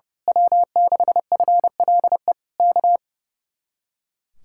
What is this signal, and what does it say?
Morse code, heard as a keyed tone, about 28 wpm, 700 Hz. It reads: W6FLE K
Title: Speed-Racing Common QSO Phrases Encore - 30wpm